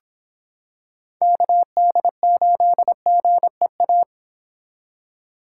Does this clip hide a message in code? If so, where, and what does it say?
Yes: Morse code from 1.2 to 4.0 s, heard as a keyed tone, about 26 wpm, 700 Hz. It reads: KD8GEA